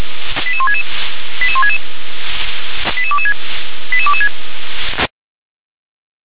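Five-tone selective call (ZVEI), heard as quick runs of five tones, tones of about 80 ms. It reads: 88166 88166 88226 88226
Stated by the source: Five-tone ZVEI selective call 88164 received by SDR.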